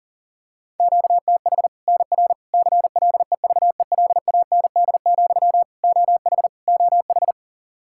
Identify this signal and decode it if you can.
Morse code, heard as a keyed tone, about 40 wpm, 700 Hz. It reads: QTH NR CLEVELAND, OH OH
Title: Practice QSO - Exchange 1 - Beginning - 40wpm